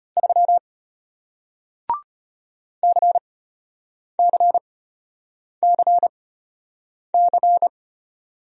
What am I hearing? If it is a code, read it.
Morse code, heard as a keyed tone, about 33 wpm, spaced as for 8 wpm, 700 Hz. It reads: 3 CCCC